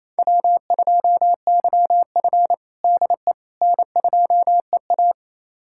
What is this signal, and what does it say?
Morse code, heard as a keyed tone, about 28 wpm, 700 Hz. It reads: W2YF DE N2EA
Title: USA Callsign DE Callsign - 4 Characters - 28wpm